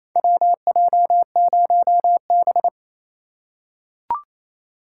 Morse code, heard as a keyed tone, about 28 wpm, 700 Hz. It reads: WJ0B
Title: USA Call Signs - 4 Characters - 4x Character Spacing - 28wpm